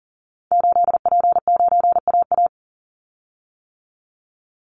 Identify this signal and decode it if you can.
Morse code, heard as a keyed tone, about 40 wpm, 700 Hz. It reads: 8P9AA
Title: Call Signs - 40wpm